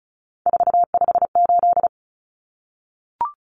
Morse code, heard as a keyed tone, about 34 wpm, 700 Hz. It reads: 458
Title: Numbers - 3-Digit 35wpm